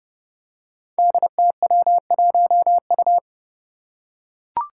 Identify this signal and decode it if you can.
Morse code, heard as a keyed tone, about 30 wpm, 700 Hz. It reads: DTW1U
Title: Five Challenging Letter-Number Combinations 30wpm